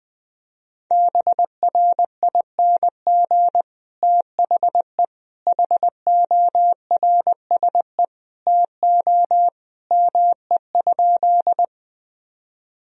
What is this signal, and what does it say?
Morse code, heard as a keyed tone, about 21 wpm, 700 Hz. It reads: BRING THE HORSE TO ME?
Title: Sentences from Top 1000 Words - Review Part 3 - 20wpm